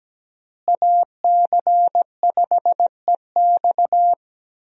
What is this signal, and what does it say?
Morse code, heard as a keyed tone, about 17 wpm, 700 Hz. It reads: AC5EX
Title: USA Call Signs - 5 Characters - Repeated 2x - 17wpm